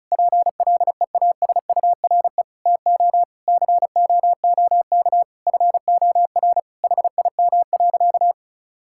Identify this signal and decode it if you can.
Morse code, heard as a keyed tone, about 34 wpm, 700 Hz. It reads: PLEASURE TO COOK FOR HIM.
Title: Sentences from Top 1600 Words 35wpm